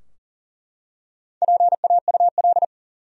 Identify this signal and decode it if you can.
Morse code, heard as a keyed tone, about 40 wpm, 700 Hz. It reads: PAUL
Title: USA Common Names - 40wpm